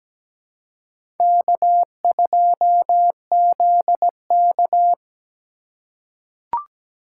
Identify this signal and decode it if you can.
Morse code, heard as a keyed tone, about 17 wpm, 700 Hz. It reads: K2ZK
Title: USA Call Signs - 4x Character Spacing - 17wpm